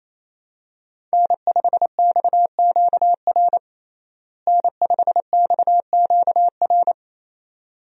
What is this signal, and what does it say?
Morse code, heard as a keyed tone, about 28 wpm, 700 Hz. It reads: N5XQR N5XQR
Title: USA Call Signs - 5 Characters - Repeated 2x - 28wpm